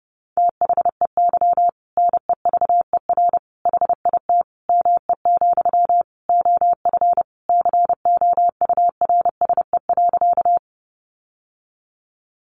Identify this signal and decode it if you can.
Morse code, heard as a keyed tone, about 30 wpm, 700 Hz. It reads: THEY NEVER HIT ME, OF COURSE.
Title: Sentences from Top 1200 Words 30wpm